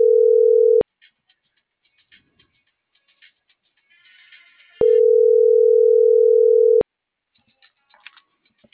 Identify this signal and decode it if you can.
Ringback tone (North America), 440 + 480 Hz.